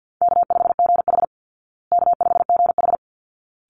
Morse code, heard as a keyed tone, about 48 wpm, 700 Hz. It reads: K5GH K5GH
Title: Call Signs Repeated 3x - 50wpm